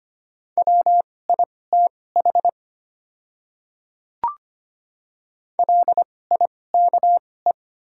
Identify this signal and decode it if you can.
Morse code, heard as a keyed tone, about 25 wpm, spaced as for 18 wpm, 700 Hz. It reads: WITH LIKE
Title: 2x Character Spacing - Top 100 Words - 25wpm